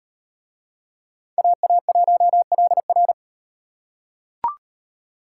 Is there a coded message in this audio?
Yes: Morse code from 1.4 to 3.1 s, heard as a keyed tone, about 37 wpm, 700 Hz. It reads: AA1LR